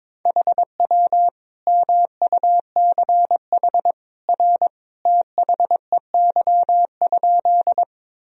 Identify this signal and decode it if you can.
Morse code, heard as a keyed tone, about 21 wpm, 700 Hz. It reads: HW MUCH R THEY?
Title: Sentences from Top 200 Words and Abbreviations 22wpm